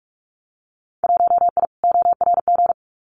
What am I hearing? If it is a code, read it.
Morse code, heard as a keyed tone, about 44 wpm, 700 Hz. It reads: 1I ORG
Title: Winter Field Day Exchanges 45wpm